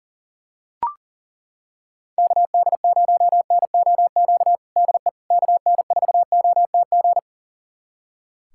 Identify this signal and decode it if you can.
Morse code, heard as a keyed tone, about 37 wpm, 700 Hz. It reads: KD0NOQ DE KN4OTG
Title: USA Callsign DE Callsign - 6 Characters - 40wpm